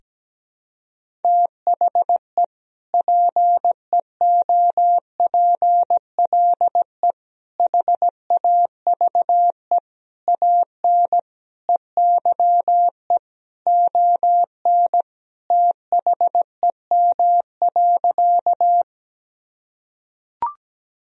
Morse code, heard as a keyed tone, about 17 wpm, 700 Hz. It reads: THE PEOPLE HAVE AN EYE ON THEM.
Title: Sentences from Top 1000 Words - Review Part 1 - 17wpm